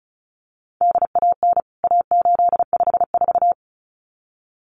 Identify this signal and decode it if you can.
Morse code, heard as a keyed tone, about 34 wpm, 700 Hz. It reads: DAN A854
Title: CWT Exchange 35wpm